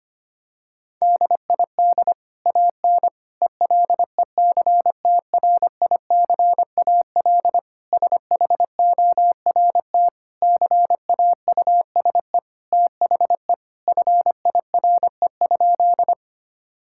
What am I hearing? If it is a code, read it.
Morse code, heard as a keyed tone, about 25 wpm, 700 Hz. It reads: DID AN ELECTRICAL SHORT CAUSE THE FIRE?
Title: Sentences from Top 1000 Words - Review Part 6 - 25wpm